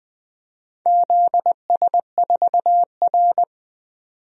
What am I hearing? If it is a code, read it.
Morse code, heard as a keyed tone, about 20 wpm, 700 Hz. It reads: ZS4R
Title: DXCC-100 Call Signs - 4 Characters - 20wpm